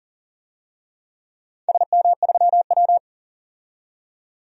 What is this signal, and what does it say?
Morse code, heard as a keyed tone, about 38 wpm, 700 Hz. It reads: SM3W
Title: DXCC-100 Call Signs - 4 Characters - 40wpm